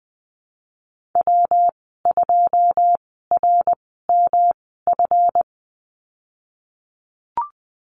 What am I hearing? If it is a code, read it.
Morse code, heard as a keyed tone, about 20 wpm, spaced as for 15 wpm, 700 Hz. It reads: W2RMF
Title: 147 Lesson 24g - Letter y - Call Signs 20wpm